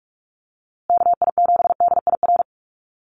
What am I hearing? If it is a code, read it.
Morse code, heard as a keyed tone, about 44 wpm, 700 Hz. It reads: KI7DIR
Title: USA Call Signs Repeated 3x 45wpm